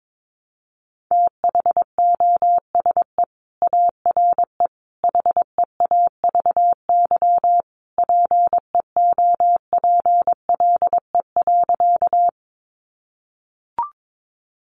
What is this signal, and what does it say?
Morse code, heard as a keyed tone, about 22 wpm, 700 Hz. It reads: THOSE ARE HEAVY PEOPLE.
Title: Sentences from Top 1100 Words 22wpm